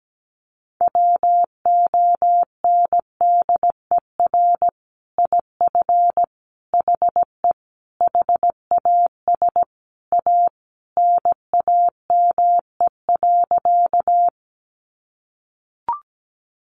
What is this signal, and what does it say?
Morse code, heard as a keyed tone, about 17 wpm, 700 Hz. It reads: WONDER IF HE HAS A NAME.
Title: Sentences from Top 500 Words 17wpm